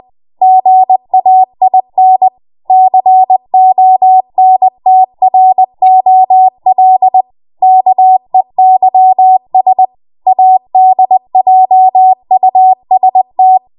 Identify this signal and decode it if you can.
Morse code, heard as a keyed tone, about 20 wpm, 750 Hz. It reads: GAIN CONTROL KEYS ADJUST
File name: Morse / 20WPM